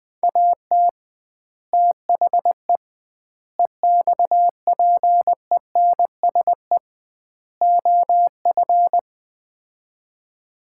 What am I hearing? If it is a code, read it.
Morse code, heard as a keyed tone, about 20 wpm, 700 Hz. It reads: AT THE EXPENSE OF
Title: 2x Word-Spacing - Sets of 4 Words - 20wpm